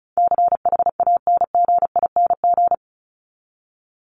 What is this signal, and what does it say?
Morse code, heard as a keyed tone, about 34 wpm, 700 Hz. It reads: CHANGING
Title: ING Suffix - Top 100 Words with Base Words 35wpm